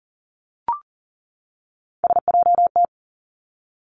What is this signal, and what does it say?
Morse code, heard as a keyed tone, about 38 wpm, 700 Hz. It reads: SJT
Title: Three Challenging Letter-Number Combinations 40wpm